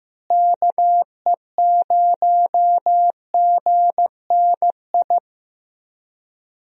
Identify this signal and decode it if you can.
Morse code, heard as a keyed tone, about 15 wpm, 700 Hz. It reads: KE0GNI
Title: USA Call Signs - 6 Characters - Repeated 2x - 15wpm